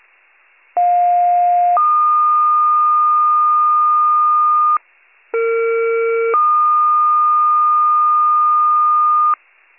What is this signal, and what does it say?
Two-tone page, heard as a one-second tone, then a three-second tone, about 710 then 1220 Hz, about 460 then 1220 Hz. It is heard from 0.8 to 9.3 s.